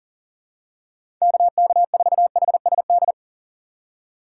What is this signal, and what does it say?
Morse code, heard as a keyed tone, about 40 wpm, 700 Hz. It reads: KK4HSD
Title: USA Call Signs - 4x Character Spacing - 40wpm